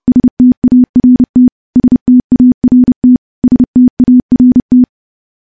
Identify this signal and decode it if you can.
Morse code, heard as a keyed tone, about 30 wpm, 250 Hz. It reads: START START START
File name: morse / morse-training-30WPM